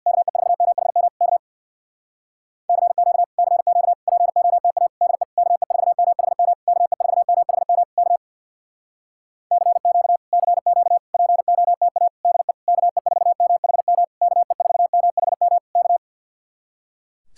Morse code, heard as a keyed tone, about 64 wpm, 700 Hz. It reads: KE4MHM K CQ CQ POTA DE KE4MHM KE4MHM K CQ CQ POTA DE KE4MHM KE4MHM K